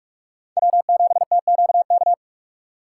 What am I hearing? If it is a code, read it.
Morse code, heard as a keyed tone, about 41 wpm, 700 Hz. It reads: W7TQK